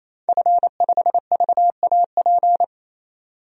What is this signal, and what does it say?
Morse code, heard as a keyed tone, about 27 wpm, 700 Hz. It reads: F5VAP
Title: DXCC-100 Call Signs - 5 Characters - 28wpm